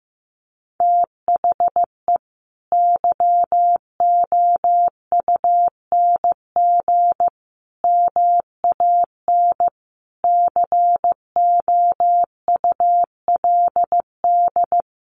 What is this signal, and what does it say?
Morse code, heard as a keyed tone, about 15 wpm, 700 Hz. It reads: THE YOUNG MAN COULD
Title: Sentences from Top 1400 Words 15wpm